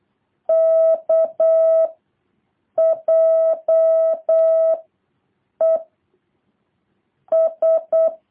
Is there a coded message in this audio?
Yes: Morse code from 0.5 to 8.1 s, heard as a keyed tone, about 8 wpm, 650 Hz. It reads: K J E S